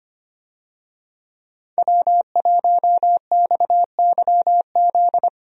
Morse code, heard as a keyed tone, about 25 wpm, 700 Hz. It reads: W1XYZ
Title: USA Call Signs - 5 Characters - 25wpm